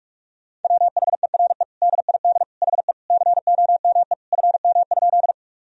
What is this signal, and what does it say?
Morse code, heard as a keyed tone, about 47 wpm, 700 Hz. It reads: WHERE DID HE COME FM?
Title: Sentences from Top 200 Words and Abbreviations 45wpm